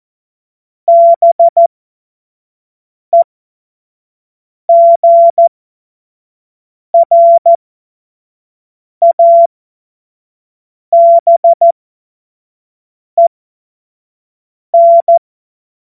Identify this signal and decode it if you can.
Morse code, heard as a keyed tone, about 14 wpm, spaced as for 5 wpm, 680 Hz. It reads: BEGRABEN
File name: Morsekurs / Witze / Witz 02. August 2022 25 F70